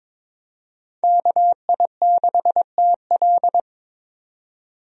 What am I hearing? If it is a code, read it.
Morse code, heard as a keyed tone, about 22 wpm, 700 Hz. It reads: KI6TL
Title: USA Call Signs - 5 Characters - 22wpm